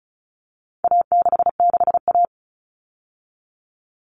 Morse code, heard as a keyed tone, about 34 wpm, 700 Hz. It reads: A66A
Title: All Cut Numbers - 4-Digit 35wpm